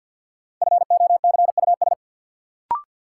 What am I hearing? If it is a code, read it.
Morse code, heard as a keyed tone, about 52 wpm, 700 Hz. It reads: FOCUS
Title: Sentences from Top 1000 Words - Review Part 4 - New Word Drill - 50wpm